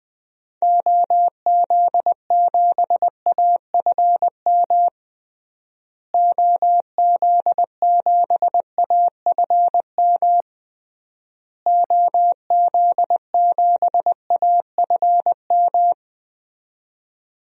Morse code, heard as a keyed tone, about 21 wpm, 700 Hz. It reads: OZ7AFM OZ7AFM OZ7AFM
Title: DXCC-100 Call Signs - 6 Characters - Repeated 3x - 20wpm